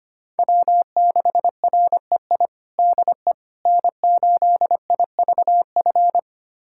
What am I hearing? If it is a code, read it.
Morse code, heard as a keyed tone, about 25 wpm, 700 Hz. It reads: W6REI DE N8IVF
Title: USA Callsign DE Callsign 25wpm